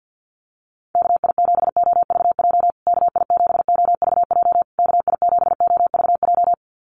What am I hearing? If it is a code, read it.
Morse code, heard as a keyed tone, about 48 wpm, 700 Hz. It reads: KI7OVJ KI7OVJ KI7OVJ
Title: USA Call Signs Repeated 3x 50wpm